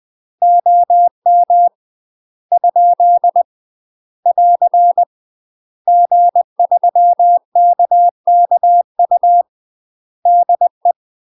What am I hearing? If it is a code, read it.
Morse code, heard as a keyed tone, about 20 wpm, 700 Hz. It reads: OM ? <AR> G3KKU DE
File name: UK MASTER QSOs TRAINING / 20WPM / UK MASTER QSOs 0073